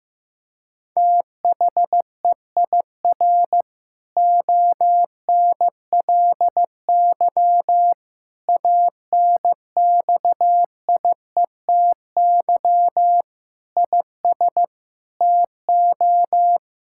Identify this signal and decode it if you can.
Morse code, heard as a keyed tone, about 15 wpm, 700 Hz. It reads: THEIR ONLY ANXIETY IS TO
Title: Sentences from Top 1600 Words 15wpm